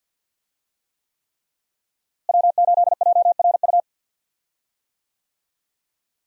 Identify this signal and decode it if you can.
Morse code, heard as a keyed tone, about 50 wpm, 700 Hz. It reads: W8JRU
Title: USA Call Signs - 5 Characters - Repeated 3x - 50wpm